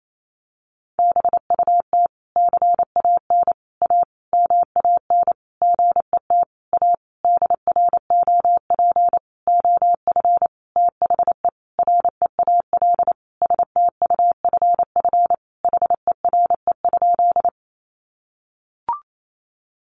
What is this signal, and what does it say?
Morse code, heard as a keyed tone, about 28 wpm, 700 Hz. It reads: BUT CAN A MAN GET A DROP OF THE REAL STUFF HERE?
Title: Sentences from Top 1400 Words 28wpm